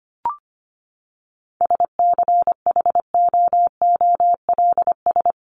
Morse code, heard as a keyed tone, about 25 wpm, 700 Hz. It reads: SCHOOLS
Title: ES:S Suffix - Top 100 Words with Base Words 25wpm